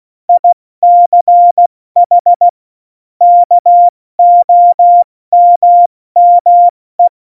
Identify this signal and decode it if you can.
Morse code, heard as a keyed tone, about 16 wpm, 700 Hz. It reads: ICH KOMME
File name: Morsekurs / Witze / Witz 16. Januar 2018, 70 F80